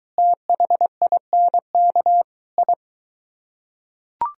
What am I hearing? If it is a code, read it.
Morse code, heard as a keyed tone, about 23 wpm, 700 Hz. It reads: THINK I